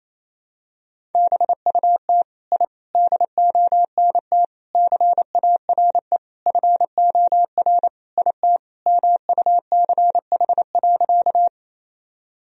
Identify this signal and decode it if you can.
Morse code, heard as a keyed tone, about 28 wpm, 700 Hz. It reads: BUT I DONT CARE FOR IT MUCH.
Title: Sentences from Top 1000 Words - Review Part 1 - 28wpm